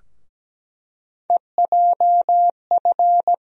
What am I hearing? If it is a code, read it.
Morse code, heard as a keyed tone, about 17 wpm, 700 Hz. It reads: EJF
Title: USA Call Sign Postfixes - 3 Letters 17wpm